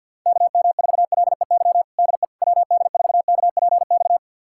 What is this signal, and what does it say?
Morse code, heard as a keyed tone, about 51 wpm, 700 Hz. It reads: KM4LEY DE WD4KPX
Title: USA Callsign DE Callsign 50wpm